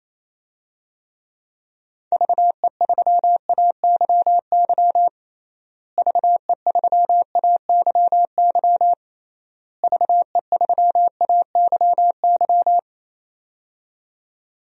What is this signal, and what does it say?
Morse code, heard as a keyed tone, about 28 wpm, 700 Hz. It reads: VE3AYY VE3AYY VE3AYY
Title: DXCC-100 Call Signs - 6 Characters - Repeated 3x - 28wpm